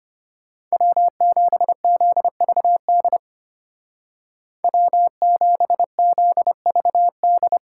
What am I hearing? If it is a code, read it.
Morse code, heard as a keyed tone, about 30 wpm, 700 Hz. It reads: W7ZVD W7ZVD
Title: Speed Racing - USA 1x3 Call Signs 25wpm